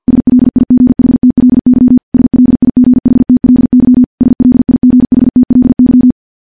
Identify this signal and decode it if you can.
Morse code, heard as a keyed tone, about 50 wpm, 250 Hz. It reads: SLIGHTLY SLIGHTLY SLIGHTLY
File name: morse / morse-training-50WPM